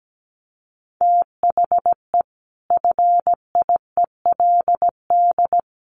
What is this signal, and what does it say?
Morse code, heard as a keyed tone, about 17 wpm, 700 Hz. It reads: THE FIELD
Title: Sentences from Top 1300 Words 17wpm